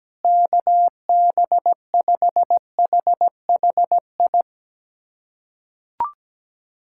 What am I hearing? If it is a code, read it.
Morse code, heard as a keyed tone, about 17 wpm, 700 Hz. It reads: KB5HHI
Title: USA Callsign DE Callsign - 6 Characters - 17wpm